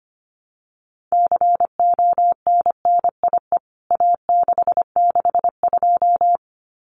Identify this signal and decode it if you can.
Morse code, heard as a keyed tone, about 25 wpm, 700 Hz. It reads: CONNIE A662
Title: CWT Exchange 25wpm